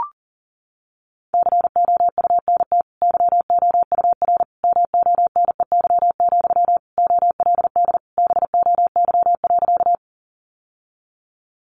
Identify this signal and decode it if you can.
Morse code, heard as a keyed tone, about 40 wpm, 700 Hz. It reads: COUNT YOUR MONEY, OLD BOY.
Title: Sentences from Top 1500 Words 40wpm